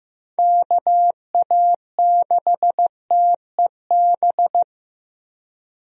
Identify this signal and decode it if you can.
Morse code, heard as a keyed tone, about 16 wpm, 700 Hz. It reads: KA6TEB
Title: USA Call Signs - 6 Characters - Repeated 3x - 15wpm